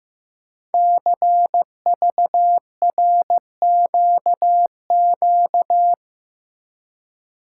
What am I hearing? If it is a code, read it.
Morse code, heard as a keyed tone, about 15 wpm, 700 Hz. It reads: CVRQQ
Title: Code Groups - 5 Letters - 15wpm